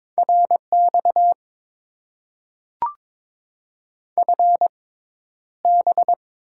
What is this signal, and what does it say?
Morse code, heard as a keyed tone, about 22 wpm, spaced as for 11 wpm, 700 Hz. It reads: RX FB
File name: Top 90 QSO Elements - 6x Character Spacing - 22wpm